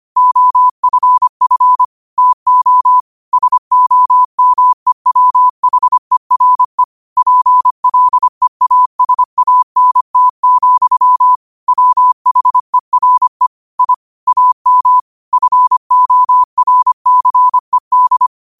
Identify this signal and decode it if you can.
Morse code, heard as a keyed tone, about 25 wpm, 1000 Hz. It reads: OFF TO SOMEWHERE PLEASANT, WHERE I AM FORCED
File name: qotdfiles / 25wpm